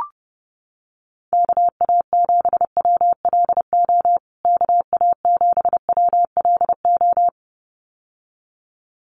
Morse code, heard as a keyed tone, about 30 wpm, 700 Hz. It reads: KA7WLO KA7WLO
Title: USA Call Signs Repeated 2x 30wpm